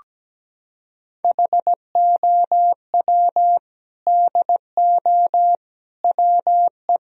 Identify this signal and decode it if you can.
Morse code, heard as a keyed tone, about 17 wpm, 700 Hz. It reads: HOW DO WE
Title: Sentences from Top 1000 Words - Review Part 1 - 17wpm